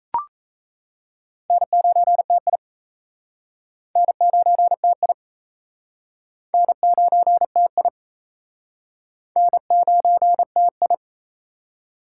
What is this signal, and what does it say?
Morse code, heard as a keyed tone, about 36 wpm, 700 Hz. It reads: N9TI N9TI N9TI N9TI